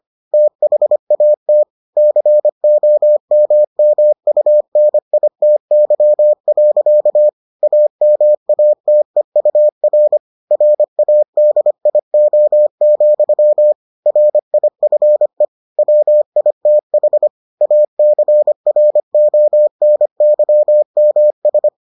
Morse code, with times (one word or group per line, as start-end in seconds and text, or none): 0.3-1.6 THAT
2.0-7.3 COMMUNITY.
7.6-10.2 AMATEUR
10.5-13.7 RADIO,
14.1-15.4 RIFE
15.8-17.3 WITH
17.6-21.7 ACRONYMS